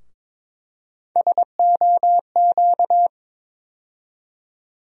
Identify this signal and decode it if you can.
Morse code, heard as a keyed tone, about 22 wpm, 700 Hz. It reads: SOQ
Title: USA Call Sign Postfixes - 3 Letters 22wpm